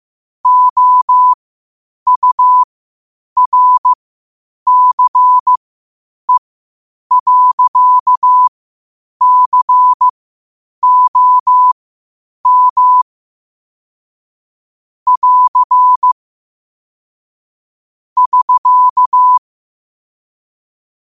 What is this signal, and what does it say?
Morse code, heard as a keyed tone, about 15 wpm, spaced as for 8 wpm, 1000 Hz. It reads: OURCE.COM <AR> <SK>
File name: qotdfiles / 8wpm